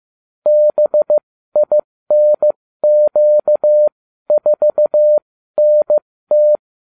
Morse code, heard as a keyed tone, about 15 wpm, spaced as for 13 wpm, 600 Hz. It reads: BINQ4NT
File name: cw / ubacw35 3